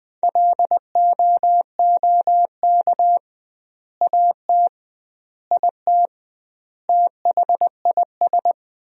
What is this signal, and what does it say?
Morse code, heard as a keyed tone, about 21 wpm, 700 Hz. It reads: LOOK AT IT THIS
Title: 2x Word-Spacing - Sentences from Top 100 Words - 20wpm